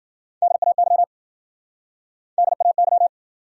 Morse code, heard as a keyed tone, about 52 wpm, 700 Hz. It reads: DAY DAY